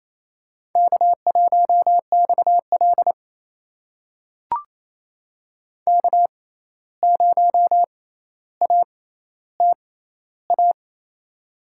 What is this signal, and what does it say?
Morse code, heard as a keyed tone, about 28 wpm, spaced as for 13 wpm, 700 Hz. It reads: K1XL K0ATA
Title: USA Call Signs - 6x Character Spacing - 28wpm